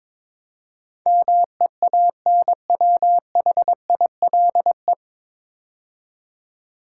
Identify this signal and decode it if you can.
Morse code, heard as a keyed tone, about 22 wpm, 700 Hz. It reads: MEANWHILE
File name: 9 Letter Words - First 3 Letters Given - 22wpm